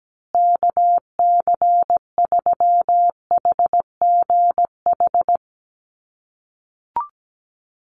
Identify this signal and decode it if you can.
Morse code, heard as a keyed tone, about 17 wpm, 700 Hz. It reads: KC3HGH
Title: USA Call Signs Repeated 2x 17wpm